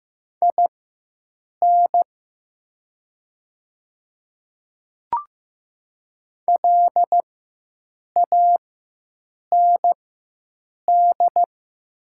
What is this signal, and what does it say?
Morse code, heard as a keyed tone, about 15 wpm, spaced as for 7 wpm, 700 Hz. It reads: IN LAND
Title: 4x Character Spacing - Top 200 Words - 15wpm